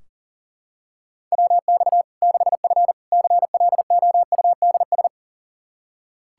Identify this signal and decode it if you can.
Morse code, heard as a keyed tone, about 40 wpm, 700 Hz. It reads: WX 6F CLOUDS
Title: Practice QSO - Exchange 2 - Info - 40wpm